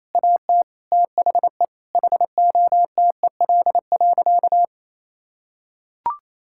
Morse code, heard as a keyed tone, about 28 wpm, 700 Hz. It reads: AT THE HOTEL.
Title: Sentences from Top 1600 Words 28wpm